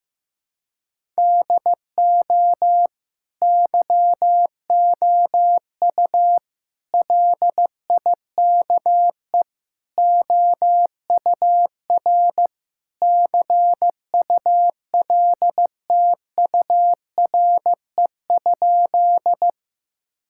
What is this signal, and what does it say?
Morse code, heard as a keyed tone, about 15 wpm, 700 Hz. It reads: DO YOU LIKE OUR CULTURE?
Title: Sentences from Top 1000 Words - Review Part 6 - 15wpm